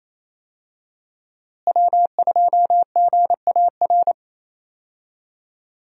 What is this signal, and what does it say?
Morse code, heard as a keyed tone, about 28 wpm, 700 Hz. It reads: W2GAR
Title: USA Call Signs - 5 Characters - Repeated 2x - 28wpm